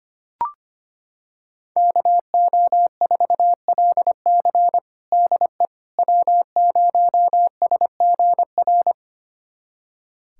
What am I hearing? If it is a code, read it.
Morse code, heard as a keyed tone, about 25 wpm, 700 Hz. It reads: KO4LC DE W0SGR
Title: USA Callsign DE Callsign - 5 Characters - 25wpm